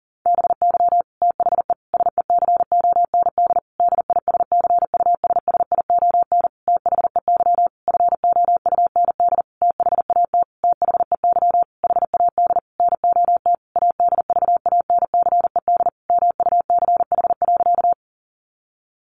Morse code, heard as a keyed tone, about 40 wpm, 700 Hz. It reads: BY THE SECOND DISCUSSION THEY FOUND THAT THEY HAD NOT ADVANCED MUCH.
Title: Sentences from Top 1200 Words 40wpm